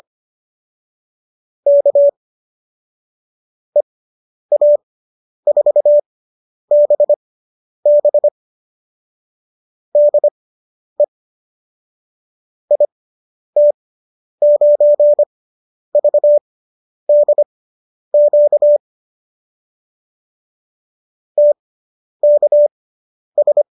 Morse code, heard as a keyed tone, about 25 wpm, spaced as for 10 wpm, 590 Hz. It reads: K EA4BB DE IT9VDQ TKS